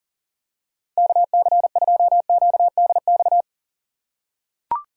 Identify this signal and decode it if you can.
Morse code, heard as a keyed tone, about 37 wpm, 700 Hz. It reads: KC2QDX
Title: USA Call Signs - 6 Characters - 6x Character Spacing - 40wpm